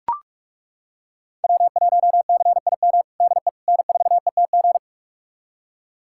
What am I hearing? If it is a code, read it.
Morse code, heard as a keyed tone, about 41 wpm, 700 Hz. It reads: W1CIM DE N4ETG
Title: USA Callsign DE Callsign - 5 Characters - 45wpm